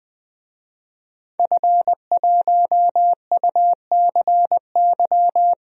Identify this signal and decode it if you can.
Morse code, heard as a keyed tone, about 21 wpm, 700 Hz. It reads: F1UCY
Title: DXCC-100 Call Signs - 5 Characters - Repeated 3x - 20wpm